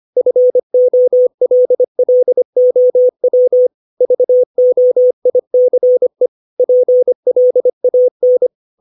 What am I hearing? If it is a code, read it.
Morse code, heard as a keyed tone, about 25 wpm, 500 Hz. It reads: FOLLOW VOICE PLAN
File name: 632 words / 632 Words 25WPM